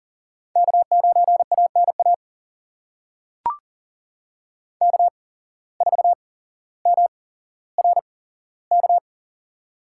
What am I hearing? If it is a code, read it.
Morse code, heard as a keyed tone, about 39 wpm, spaced as for 17 wpm, 700 Hz. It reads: K9ANA K4MRK
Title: USA Call Signs - 5 Characters - 8x Character Spacing - 40wpm